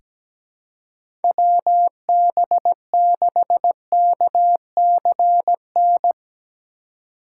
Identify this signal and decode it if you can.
Morse code, heard as a keyed tone, about 17 wpm, 700 Hz. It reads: WB6KCN
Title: USA Call Signs - 6 Characters - 3x Character Spacing - 17wpm